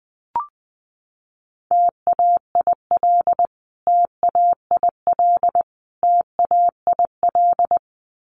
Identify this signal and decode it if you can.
Morse code, heard as a keyed tone, about 20 wpm, 700 Hz. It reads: TAIL TAIL TAIL
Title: Top 500 Words Repeated 20wpm